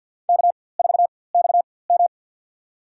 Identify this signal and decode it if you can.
Morse code, heard as a keyed tone, about 51 wpm, spaced as for 23 wpm, 690 Hz. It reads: K4XM
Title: USA Call Signs - 4x Character Spacing - 50wpm